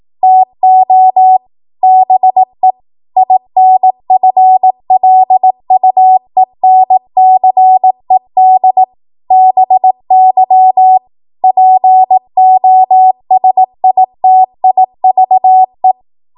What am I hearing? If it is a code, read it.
Morse code, heard as a keyed tone, about 18 wpm, 750 Hz. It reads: TO BE INFLUENCED BY POSITIVE
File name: Morse / 18WPM